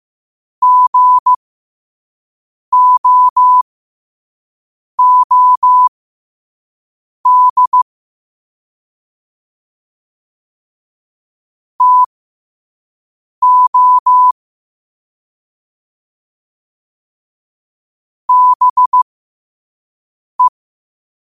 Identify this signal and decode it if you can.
Morse code, heard as a keyed tone, about 15 wpm, spaced as for 5 wpm, 1000 Hz. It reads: GOOD TO BE
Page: qotd-at-5-wpm